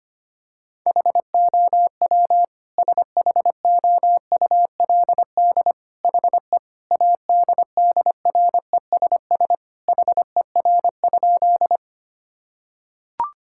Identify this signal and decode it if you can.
Morse code, heard as a keyed tone, about 25 wpm, 700 Hz. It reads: HOW SHOULD HE ADDRESS HER?
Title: Sentences from Top 1000 Words - Review Part 6 - 25wpm